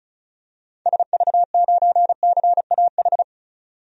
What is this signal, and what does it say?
Morse code, heard as a keyed tone, about 36 wpm, 700 Hz. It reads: SV9CAH
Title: DXCC-100 Call Signs - 6 Characters - 35wpm